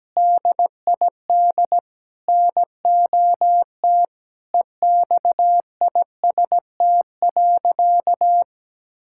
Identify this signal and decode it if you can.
Morse code, heard as a keyed tone, about 17 wpm, 700 Hz. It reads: DID NOT EXIST.
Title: Sentences from Top 1000 Words - Review Part 6 - 17wpm